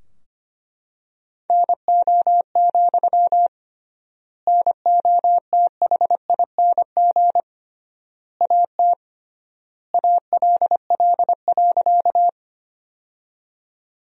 Morse code, heard as a keyed tone, about 25 wpm, 700 Hz. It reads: NO, NOTHING AT ALL.
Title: 3x Word-Spacing - Sentences from Top 500 Words - 25wpm